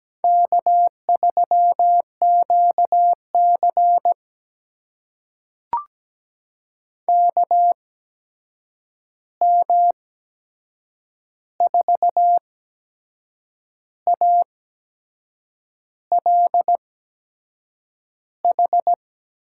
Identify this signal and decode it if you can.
Morse code, heard as a keyed tone, about 17 wpm, spaced as for 6 wpm, 700 Hz. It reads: K3QC KM4ALH
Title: USA Call Signs - 8x Character Spacing - 17wpm